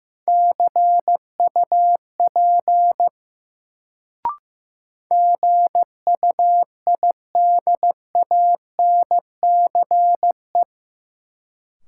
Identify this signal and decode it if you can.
Morse code, heard as a keyed tone, about 15 wpm, 700 Hz. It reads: CUP GUIDANCE
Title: Sentences from Top 1600 Words - New Word Drill Alt - 15wpm